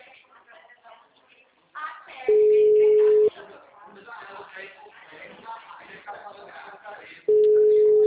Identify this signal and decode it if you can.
Ringback tone (Europe, 425 Hz), with speech.